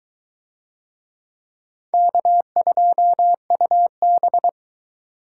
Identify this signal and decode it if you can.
Morse code, heard as a keyed tone, about 21 wpm, 700 Hz. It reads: K2UB